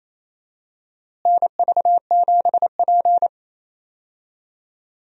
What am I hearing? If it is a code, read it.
Morse code, heard as a keyed tone, about 28 wpm, 700 Hz. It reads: NV7P
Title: USA Call Signs - 4 Characters - Repeated 3x - 28wpm